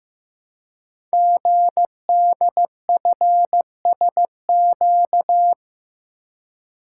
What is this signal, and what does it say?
Morse code, heard as a keyed tone, about 15 wpm, 700 Hz. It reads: GDFSQ
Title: Code Groups - 5 Letters - 15wpm